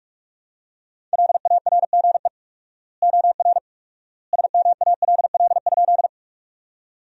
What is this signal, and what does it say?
Morse code, heard as a keyed tone, about 44 wpm, 700 Hz. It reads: LARGE OR SMALL?
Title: 4x Word-Spacing - Sentences from Top 200 Words - 45wpm